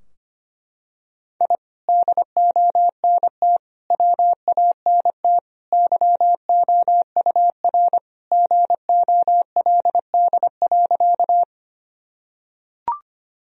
Morse code, heard as a keyed tone, about 25 wpm, 700 Hz. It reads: I DONT WANT YOUR GOLD.
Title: Sentences from Top 1000 Words - Review Part 3 - 25wpm